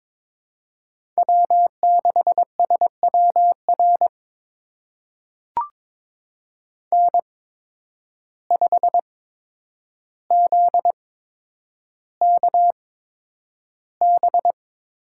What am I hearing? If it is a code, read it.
Morse code, heard as a keyed tone, about 21 wpm, spaced as for 9 wpm, 700 Hz. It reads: W6SWR N5ZKB
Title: USA Call Signs - 5 Characters - 8x Character Spacing - 22wpm